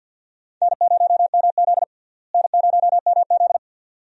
Morse code, heard as a keyed tone, about 51 wpm, 700 Hz. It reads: N0MZ N0MZ
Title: USA Call Signs - 4 Characters - Repeated 2x - 50wpm